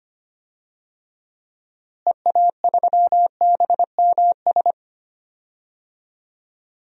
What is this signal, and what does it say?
Morse code, heard as a keyed tone, about 25 wpm, 700 Hz. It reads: EA3BMS
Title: DXCC-100 Call Signs - 6 Characters - Repeated 2x - 25wpm